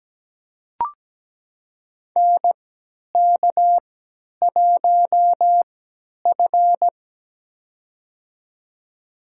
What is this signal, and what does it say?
Morse code, heard as a keyed tone, about 17 wpm, spaced as for 10 wpm, 700 Hz. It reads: NK1F
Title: USA Call Signs - 3x Character Spacing - 17wpm